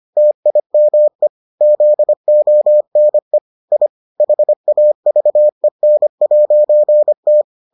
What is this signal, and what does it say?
Morse code, heard as a keyed tone, about 25 wpm, 600 Hz. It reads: TIME ZONE I HAVEN'T